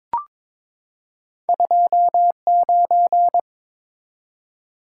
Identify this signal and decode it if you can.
Morse code, heard as a keyed tone, about 22 wpm, 700 Hz. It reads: 29
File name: practice / 22wpm / Two Letter-Number Combinations - 22wpm